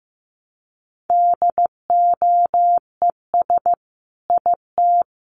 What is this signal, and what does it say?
Morse code, heard as a keyed tone, about 15 wpm, 700 Hz. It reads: DOES IT
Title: Sentences from Top 1300 Words 15wpm